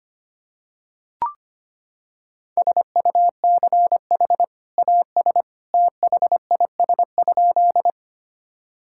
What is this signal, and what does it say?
Morse code, heard as a keyed tone, about 25 wpm, 700 Hz. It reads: SUCH AS THIS?
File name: practice / 25wpm / Sentences with 2-Word Phrases - 25wpm